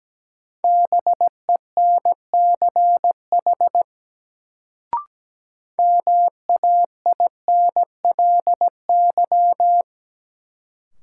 Morse code, heard as a keyed tone, about 17 wpm, 700 Hz. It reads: BENCH MAINLY
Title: Sentences from Top 1400 Words - New Word Drill Alt - 17wpm